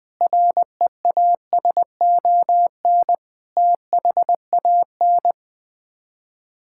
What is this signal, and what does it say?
Morse code, heard as a keyed tone, about 20 wpm, 700 Hz. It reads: REASON THAN
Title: Sets of 6 Words - 20wpm